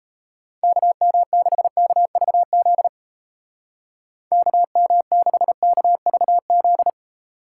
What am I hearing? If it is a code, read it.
Morse code, heard as a keyed tone, about 35 wpm, 700 Hz. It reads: KM6KVZ KM6KVZ
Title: Speed Racing - USA 2x3 Call Signs 28wpm